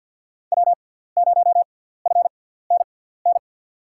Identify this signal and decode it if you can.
Morse code, heard as a keyed tone, about 52 wpm, spaced as for 17 wpm, 700 Hz. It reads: W0FNN
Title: USA Call Signs - 5 Characters - 6x Character Spacing - 50wpm